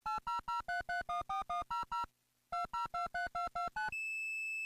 DTMF, heard as telephone touch tones, tones of about 140 ms. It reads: 8003314100 2023229